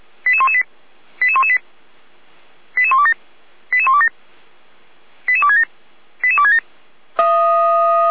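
Five-tone selective call (ZVEI), heard as quick runs of five tones, tones of about 70 ms. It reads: 88118 88118 88127 88127 88267 88267
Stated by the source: ZVEI five-tone selective call received by SDR, address 88129, from feuerwehr-orlamuende.org fire brigade server.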